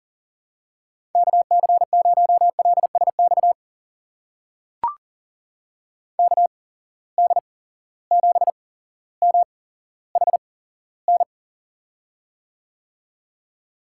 Morse code, heard as a keyed tone, about 37 wpm, spaced as for 17 wpm, 700 Hz. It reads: KC0LSX KD7MHN
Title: USA Call Signs - 6 Characters - 8x Character Spacing - 40wpm